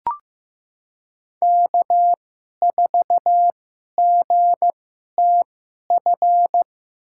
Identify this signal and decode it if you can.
Morse code, heard as a keyed tone, about 15 wpm, spaced as for 11 wpm, 700 Hz. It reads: K4GTF
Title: USA Call Signs - 2x Character Spacing - 15wpm